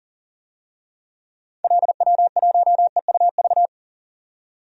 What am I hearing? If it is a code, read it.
Morse code, heard as a keyed tone, about 40 wpm, 700 Hz. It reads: LW1EUV
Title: DXCC-100 Call Signs - 6 Characters - 40wpm